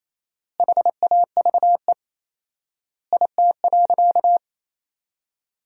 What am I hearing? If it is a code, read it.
Morse code, heard as a keyed tone, about 27 wpm, 700 Hz. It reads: HAVE IT.
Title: 4x Word-Spacing - Sentences from Top 300 Words - 28wpm